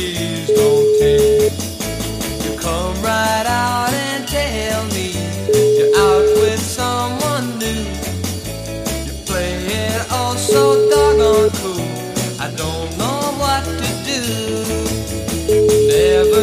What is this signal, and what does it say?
Ringback tone (Europe, 425 Hz), with noise.